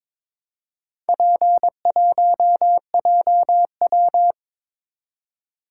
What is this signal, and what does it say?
Morse code, heard as a keyed tone, about 22 wpm, 700 Hz. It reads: P1JW
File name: practice / 22wpm / Four Letter-Number Combinations - 22wpm